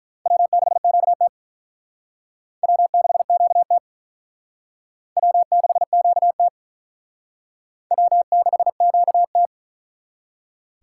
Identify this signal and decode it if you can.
Morse code, heard as a keyed tone, about 42 wpm, 700 Hz. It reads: W6QT W6QT W6QT W6QT
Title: Speed Racing - USA Call Signs 35wpm